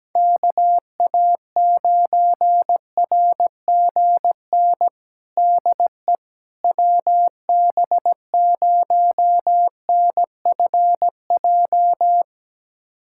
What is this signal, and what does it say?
Morse code, heard as a keyed tone, about 17 wpm, 700 Hz. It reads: KA9RGN DE WB0NFJ
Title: USA Callsign DE Callsign - 6 Characters - 17wpm